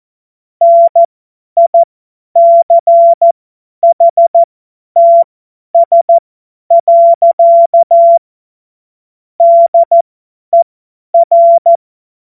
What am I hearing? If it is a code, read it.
Morse code, heard as a keyed tone, about 14 wpm, spaced as for 10 wpm, 680 Hz. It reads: NICHTS. DER
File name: Morsekurs / Witze / Witz 27. April 2021, 50 F70